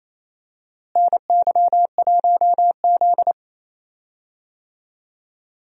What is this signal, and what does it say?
Morse code, heard as a keyed tone, about 27 wpm, 700 Hz. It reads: NY1Z
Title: USA Call Signs - 4 Characters - Repeated 3x - 28wpm